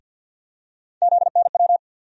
Morse code, heard as a keyed tone, about 47 wpm, 700 Hz. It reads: ZNW
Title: USA Call Sign Postfixes - 3 Letters 50wpm